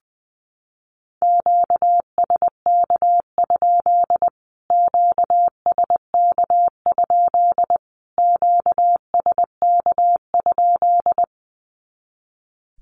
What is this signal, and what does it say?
Morse code, heard as a keyed tone, about 20 wpm, 700 Hz. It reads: QSK? QSK? QSK?
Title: Q-Codes Repeated 3x 20wpm